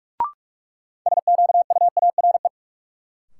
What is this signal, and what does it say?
Morse code, heard as a keyed tone, about 44 wpm, 700 Hz. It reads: SQUARE
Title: Sentences from Top 1000 Words - Review Part 7 - New Word Drill - 45wpm